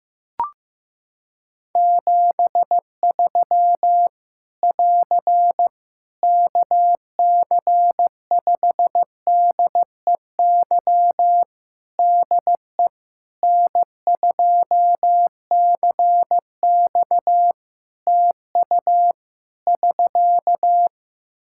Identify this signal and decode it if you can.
Morse code, heard as a keyed tone, about 15 wpm, 700 Hz. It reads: 73 <AR> KC5DEY DE N2CX TU <SK>
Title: Practice QSO - Exchange 4 - Ending - 15wpm